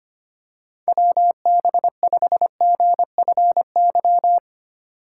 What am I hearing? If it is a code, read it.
Morse code, heard as a keyed tone, about 25 wpm, 700 Hz. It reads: WB5GFY
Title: USA Call Signs - 3x Character Spacing - 25wpm